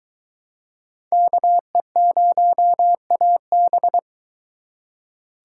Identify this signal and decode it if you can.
Morse code, heard as a keyed tone, about 23 wpm, 700 Hz. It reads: KE0AB